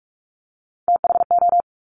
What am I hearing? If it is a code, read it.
Morse code, heard as a keyed tone, about 46 wpm, 700 Hz. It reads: THO
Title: Top Three Letter Combinations in Words 45wpm